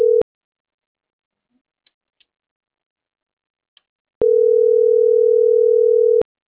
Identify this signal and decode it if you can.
Ringback tone (North America), 440 + 480 Hz.